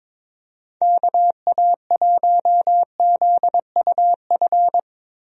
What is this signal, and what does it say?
Morse code, heard as a keyed tone, about 22 wpm, 700 Hz. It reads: KA1ZUF
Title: USA Call Signs - 6 Characters - 22wpm